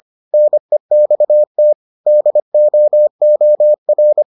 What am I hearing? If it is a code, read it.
Morse code, heard as a keyed tone, about 25 wpm, 600 Hz. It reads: NEXT DOOR